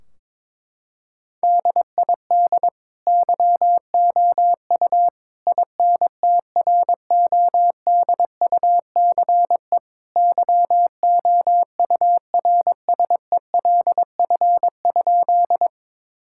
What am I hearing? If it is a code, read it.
Morse code, heard as a keyed tone, about 22 wpm, 700 Hz. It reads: DID YOU INTRODUCE YOURSELF?
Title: Sentences from Top 1600 Words 22wpm